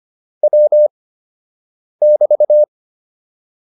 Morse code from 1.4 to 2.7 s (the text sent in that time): <BT>